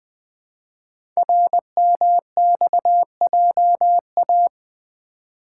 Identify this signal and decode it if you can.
Morse code, heard as a keyed tone, about 20 wpm, 700 Hz. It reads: RMXJA
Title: Code Groups - 5 Letters - 20wpm